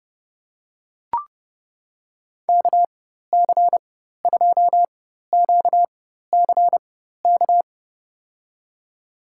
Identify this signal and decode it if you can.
Morse code, heard as a keyed tone, about 30 wpm, spaced as for 14 wpm, 700 Hz. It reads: KC2QCK
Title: USA Call Signs - 6 Characters - 4x Character Spacing - 30wpm